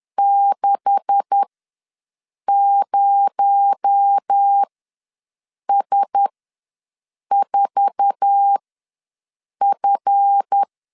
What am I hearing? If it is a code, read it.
Morse code, heard as a keyed tone, about 10 wpm, spaced as for 6 wpm, 800 Hz. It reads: 60S4F
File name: LEZIONI / 9 286 ws 10wpm 4032